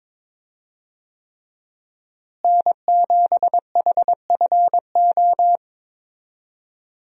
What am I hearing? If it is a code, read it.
Morse code, heard as a keyed tone, about 22 wpm, 700 Hz. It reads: N7HFO
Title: USA Call Signs - 5 Characters - Repeated 3x - 22wpm